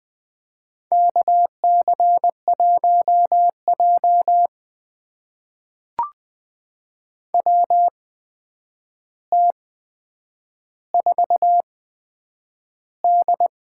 Morse code, heard as a keyed tone, about 20 wpm, spaced as for 9 wpm, 700 Hz. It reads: KC1J WT4D
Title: USA Call Signs - 4 Characters - 8x Character Spacing - 20wpm